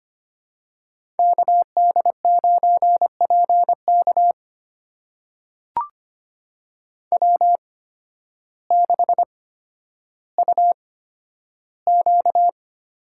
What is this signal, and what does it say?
Morse code, heard as a keyed tone, about 25 wpm, spaced as for 12 wpm, 700 Hz. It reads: KD9PK W6UQ
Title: USA Call Signs - 5 Characters - 8x Character Spacing - 25wpm